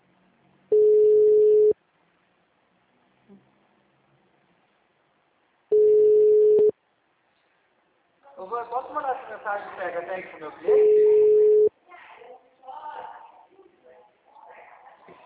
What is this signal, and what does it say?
Ringback tone (Europe, 425 Hz), with speech.